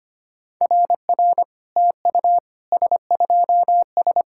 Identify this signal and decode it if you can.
Morse code, heard as a keyed tone, about 25 wpm, 690 Hz. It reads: RR TU S2S
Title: SOTA QSO Practice 25wpm